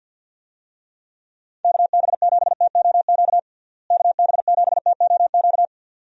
Morse code, heard as a keyed tone, about 51 wpm, 700 Hz. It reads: KB7TOQ KB7TOQ
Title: USA Call Signs - 6 Characters - Repeated 2x - 50wpm